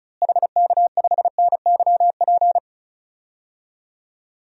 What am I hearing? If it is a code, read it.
Morse code, heard as a keyed tone, about 36 wpm, 700 Hz. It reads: HK5NYP
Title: DXCC-100 Call Signs - 6 Characters - Repeated 2x - 35wpm